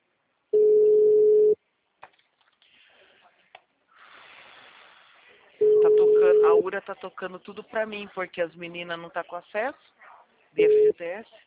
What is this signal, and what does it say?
Ringback tone (Europe, 425 Hz), with speech.